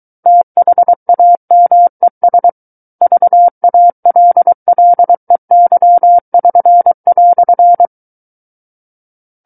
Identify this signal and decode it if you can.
Morse code, heard as a keyed tone, about 23 wpm, 700 Hz. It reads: THAMES VALLEY<SN>"
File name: Book One / 14-Chapter 14 - 23 WPM